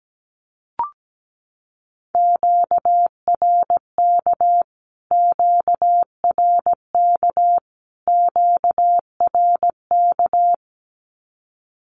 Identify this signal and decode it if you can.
Morse code, heard as a keyed tone, about 17 wpm, 700 Hz. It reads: QRK QRK QRK
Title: Q-Codes Repeated 3x 17wpm